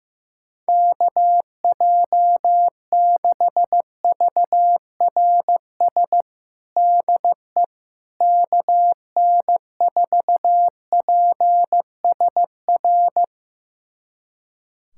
Morse code, heard as a keyed tone, about 15 wpm, 700 Hz. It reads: KJ6VRS DE KN4PSR
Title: USA Callsign DE Callsign - 6 Characters - 15wpm